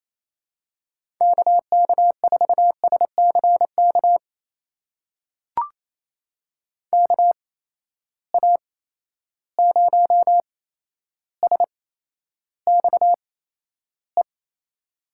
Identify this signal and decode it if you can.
Morse code, heard as a keyed tone, about 27 wpm, spaced as for 12 wpm, 700 Hz. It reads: KK4SCK KA0SXE
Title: USA Call Signs - 6 Characters - 8x Character Spacing - 28wpm